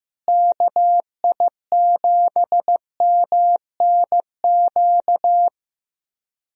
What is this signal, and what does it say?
Morse code, heard as a keyed tone, about 15 wpm, 700 Hz. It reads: KI7MNQ
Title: USA Call Signs - 6 Characters - Repeated 2x - 15wpm